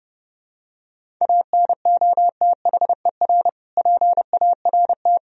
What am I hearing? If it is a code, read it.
Morse code, heard as a keyed tone, about 32 wpm, 700 Hz. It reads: ANOTHER PART
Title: Sentences from Top 1000 Words - Review Part 2 - 30wpm